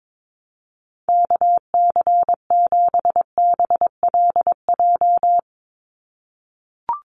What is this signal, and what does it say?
Morse code, heard as a keyed tone, about 22 wpm, 700 Hz. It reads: KC7BLJ
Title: USA Call Signs 22wpm